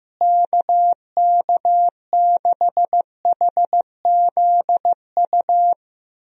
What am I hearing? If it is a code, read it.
Morse code, heard as a keyed tone, about 15 wpm, 700 Hz. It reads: KK6HZU
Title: USA Call Signs - 6 Characters - Repeated 3x - 15wpm